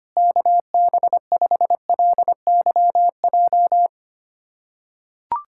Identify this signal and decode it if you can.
Morse code, heard as a keyed tone, about 25 wpm, 700 Hz. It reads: KB5LYJ
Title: USA Callsign DE Callsign 25wpm